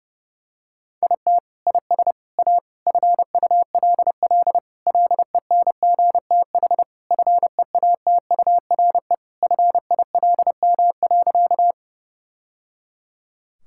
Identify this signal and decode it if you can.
Morse code, heard as a keyed tone, about 28 wpm, 700 Hz. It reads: IT IS A FULL LENGTH FEATURE FILM.
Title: Sentences from Top 1000 Words - Review Part 5 - 30wpm